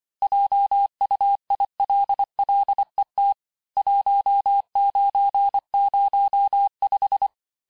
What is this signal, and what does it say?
Morse code, heard as a keyed tone, about 24 wpm, 800 Hz. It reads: JUILLET 1905
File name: cours-du-lundi files / cours-1583--texte-clair-18-puis-23-mpm